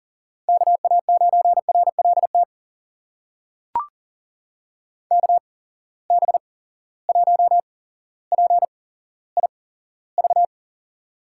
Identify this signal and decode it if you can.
Morse code, heard as a keyed tone, about 40 wpm, spaced as for 17 wpm, 700 Hz. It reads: KA9RLT KB1PIV
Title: USA Call Signs - 6 Characters - 8x Character Spacing - 40wpm